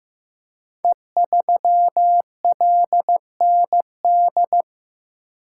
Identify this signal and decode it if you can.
Morse code, heard as a keyed tone, about 15 wpm, 700 Hz. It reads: E3LND
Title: Code Groups - 5 Letter:Numbers - 15wpm